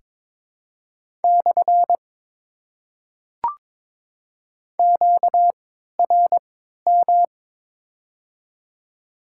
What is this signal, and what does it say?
Morse code, heard as a keyed tone, about 21 wpm, spaced as for 6 wpm, 700 Hz. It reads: /QRM
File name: Top 90 QSO Elements - 3x Character Spacing - 22wpm